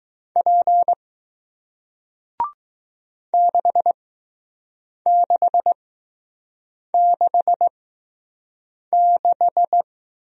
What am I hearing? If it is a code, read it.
Morse code, heard as a keyed tone, about 20 wpm, spaced as for 6 wpm, 700 Hz. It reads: P 6666